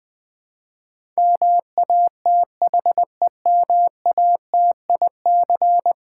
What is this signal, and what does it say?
Morse code, heard as a keyed tone, about 20 wpm, 700 Hz. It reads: MATHEMATIC
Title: Build A Word - 5 Letters At A Time - 12-Letter Words - 20wpm